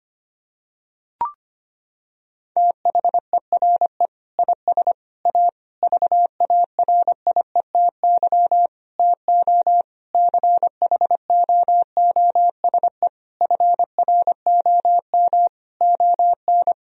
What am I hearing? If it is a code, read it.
Morse code, heard as a keyed tone, about 25 wpm, 700 Hz. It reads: THERE IS A VARIETY TO CHOOSE FROM ON
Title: Sentences from Top 1000 Words - Review Part 4 - 25wpm